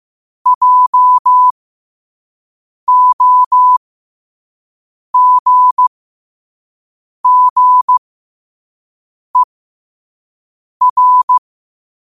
Morse code, heard as a keyed tone, about 15 wpm, spaced as for 5 wpm, 1000 Hz. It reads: JOGGER
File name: qotdfiles / 5wpm